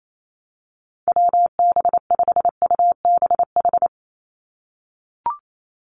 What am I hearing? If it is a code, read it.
Morse code, heard as a keyed tone, about 28 wpm, 700 Hz. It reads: WB5UBH
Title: USA Call Signs 28wpm